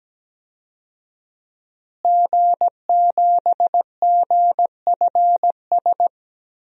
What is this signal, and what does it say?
Morse code, heard as a keyed tone, about 17 wpm, 700 Hz. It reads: G7GFS
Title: DXCC-100 Call Signs - 5 Characters - Repeated 2x - 17wpm